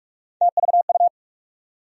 Morse code, heard as a keyed tone, about 46 wpm, 700 Hz. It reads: TVU
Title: USA Call Sign Postfixes - 3 Letters 45wpm